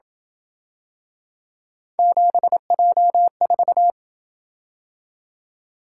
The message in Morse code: 7J4

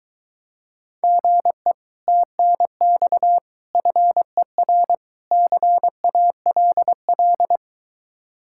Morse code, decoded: GE TNX FER CALL

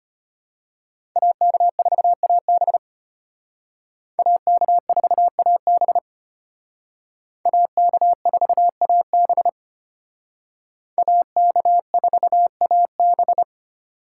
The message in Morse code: AK4AB AK4AB AK4AB AK4AB

